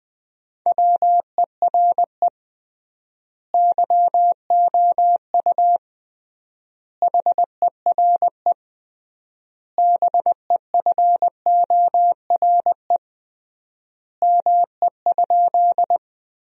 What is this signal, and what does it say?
Morse code, heard as a keyed tone, about 20 wpm, 700 Hz. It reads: WERE YOU HERE BEFORE ME?
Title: 3x Word-Spacing - Sentences from Top 200 Words - 20wpm